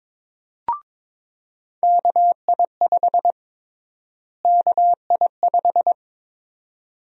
Morse code, heard as a keyed tone, about 22 wpm, 700 Hz. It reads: KI5 KI5
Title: USA Call Sign Prefixes - Repeated 2x - 22wpm